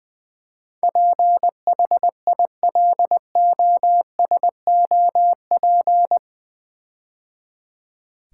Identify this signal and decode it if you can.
Morse code, heard as a keyed tone, about 20 wpm, 700 Hz. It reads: PHILOSOP
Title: Build A Word - 4 Letters At A Time - 10-Letter Words - 20wpm